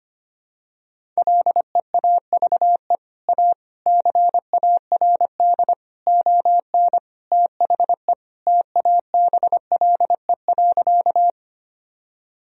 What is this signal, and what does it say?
Morse code, heard as a keyed tone, about 25 wpm, 700 Hz. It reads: LEAVE A CARD ON THE TABLE.
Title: Sentences from Top 1000 Words - Review Part 4 - 25wpm